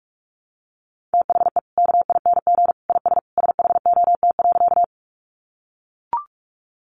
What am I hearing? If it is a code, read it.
Morse code, heard as a keyed tone, about 44 wpm, 700 Hz. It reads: THE KING IS SHOT.
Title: Sentences from Top 1100 Words 45wpm